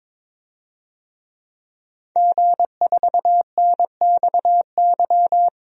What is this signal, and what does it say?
Morse code, heard as a keyed tone, about 22 wpm, 700 Hz. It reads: G4NXY
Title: DXCC-100 Call Signs - 5 Characters - 22wpm